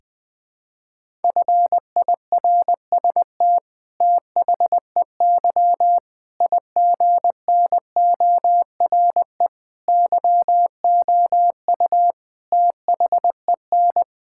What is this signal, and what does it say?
Morse code, heard as a keyed tone, about 20 wpm, 700 Hz. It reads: FIRST THEY IGNORE YOU THEN
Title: Sentences with 8-Word Phrases - 20wpm